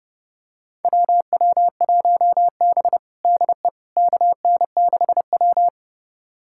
Morse code, heard as a keyed tone, about 30 wpm, 700 Hz. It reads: WW1B DE KN6W
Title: USA Callsign DE Callsign - 4 Characters - 30wpm